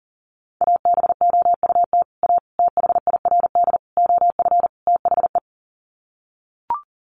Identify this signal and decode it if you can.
Morse code, heard as a keyed tone, about 40 wpm, 700 Hz. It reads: ABOUT A THIRD OF THE
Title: Sets of 5 Words Encore2 40wpm